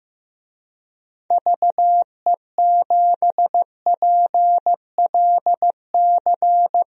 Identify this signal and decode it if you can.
Morse code, heard as a keyed tone, about 16 wpm, 700 Hz. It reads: VE7PLC
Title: DXCC-100 Call Signs - 6 Characters - 15wpm